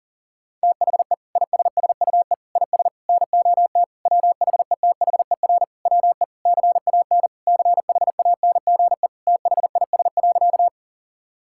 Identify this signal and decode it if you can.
Morse code, heard as a keyed tone, about 40 wpm, 700 Hz. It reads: THE ISSUE IS NOT WHETHER WE CAN CHANGE THIS.
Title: Sentences from Top 1000 Words - Review Part 6 - 40wpm